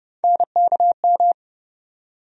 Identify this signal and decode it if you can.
Morse code, heard as a keyed tone, about 30 wpm, 700 Hz. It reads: NKM
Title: USA Call Sign Postfixes - 3 Letters 30wpm